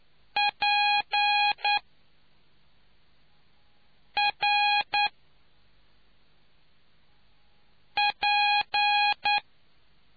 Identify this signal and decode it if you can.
Morse code, heard as a keyed tone, about 9 wpm, spaced as for 3 wpm, 830 Hz. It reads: PRP